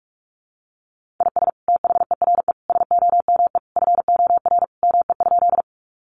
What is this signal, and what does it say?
Morse code, heard as a keyed tone, about 44 wpm, 700 Hz. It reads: IS THERE SOME FOR ME?
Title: Sentences from Top 200 Words 45wpm